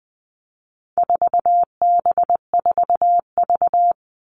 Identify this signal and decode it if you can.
Morse code, heard as a keyed tone, about 20 wpm, 700 Hz. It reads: 4B4V
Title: All Cut Numbers - 4-Digit 20wpm